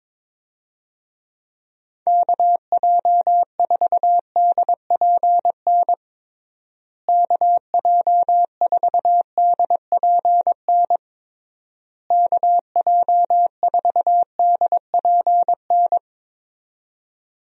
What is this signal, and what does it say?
Morse code, heard as a keyed tone, about 22 wpm, 700 Hz. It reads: KJ4DPN KJ4DPN KJ4DPN
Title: USA Call Signs - 6 Characters - Repeated 3x - 22wpm